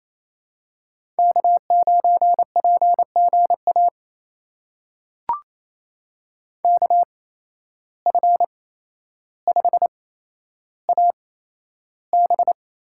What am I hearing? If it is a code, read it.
Morse code, heard as a keyed tone, about 28 wpm, spaced as for 12 wpm, 700 Hz. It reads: K9PGA KF5AB
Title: USA Call Signs - 5 Characters - 8x Character Spacing - 28wpm